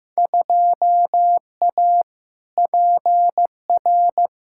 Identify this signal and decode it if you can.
Morse code, heard as a keyed tone, about 15 wpm, 700 Hz. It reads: 2A PR
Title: Field Day Exchanges Encore - 15wpm